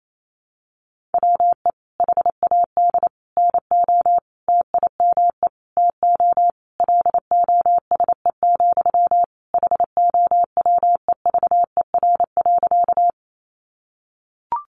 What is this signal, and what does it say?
Morse code, heard as a keyed tone, about 28 wpm, 700 Hz. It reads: WE HAD NO TIME TO LOSE, HOWEVER.
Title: Sentences from Top 1100 Words 28wpm